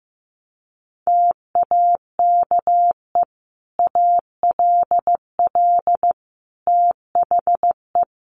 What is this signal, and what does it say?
Morse code, heard as a keyed tone, about 15 wpm, 700 Hz. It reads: TAKE ALL THE
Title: Sentences from Top 900 Words 15wpm